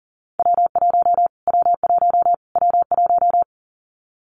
Morse code, heard as a keyed tone, about 40 wpm, 700 Hz. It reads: W1 W1 W1
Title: Callsign Prefixes for Top 25 DXCC - Repeated 3x - 40wpm